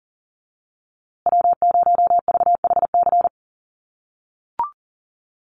Morse code, heard as a keyed tone, about 40 wpm, 700 Hz. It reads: W0VHC
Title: USA Call Signs Repeated 3x 40wpm